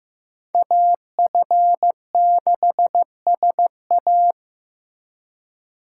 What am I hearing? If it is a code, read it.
Morse code, heard as a keyed tone, about 18 wpm, 700 Hz. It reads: AF6SA
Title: Speed Racing - USA 2x2 Call Signs 15wpm